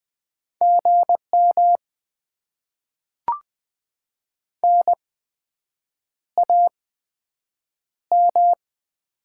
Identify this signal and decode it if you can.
Morse code, heard as a keyed tone, about 20 wpm, spaced as for 7 wpm, 700 Hz. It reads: GM NAM
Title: Top 20 QSO Elements - 8x Character Spacing - 20wpm